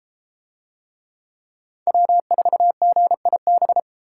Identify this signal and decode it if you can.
Morse code, heard as a keyed tone, about 31 wpm, 700 Hz. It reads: W4GIB